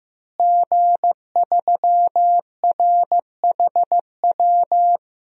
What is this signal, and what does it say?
Morse code, heard as a keyed tone, about 15 wpm, 700 Hz. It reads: G3RHW
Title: DXCC-100 Call Signs - 5 Characters - Repeated 3x - 15wpm